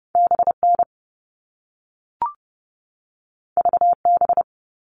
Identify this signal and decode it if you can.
Morse code, heard as a keyed tone, about 30 wpm, 700 Hz. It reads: BN VB
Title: All Cut Numbers - 2-Digit 30wpm